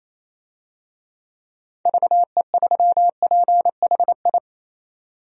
Morse code, heard as a keyed tone, about 27 wpm, 700 Hz. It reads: VE3PHI
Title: DXCC-100 Call Signs - 6 Characters - 28wpm